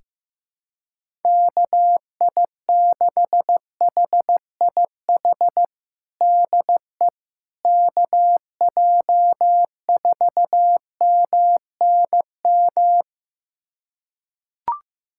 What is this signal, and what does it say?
Morse code, heard as a keyed tone, about 16 wpm, 700 Hz. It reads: KI6HIH DE KJ4MNM
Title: USA Callsign DE Callsign 15wpm